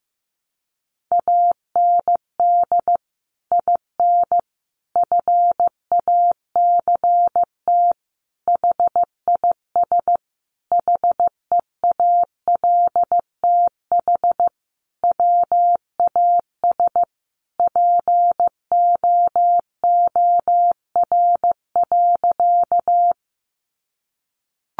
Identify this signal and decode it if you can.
Morse code, heard as a keyed tone, about 15 wpm, 700 Hz. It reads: AND IN FACT HIS HEALTH WAS POOR.
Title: Sentences from Top 1000 Words 15wpm